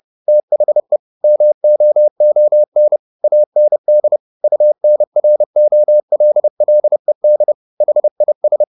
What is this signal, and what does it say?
Morse code, heard as a keyed tone, about 30 wpm, 600 Hz. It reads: THE MOON AND UNROLLED HIS
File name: the old man and the sea / the old man and the sea 1 30wpm